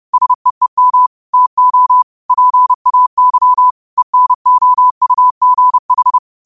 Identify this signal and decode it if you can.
Morse code, heard as a keyed tone, about 30 wpm, 1000 Hz. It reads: SEEM TO PAY ENOUGH
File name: qotdfiles / 30wpm